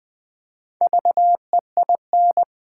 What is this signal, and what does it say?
Morse code, heard as a keyed tone, about 20 wpm, 700 Hz. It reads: VEIN
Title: Build A Word - 1 Letter At A Time - 4-Letter Words - 20wpm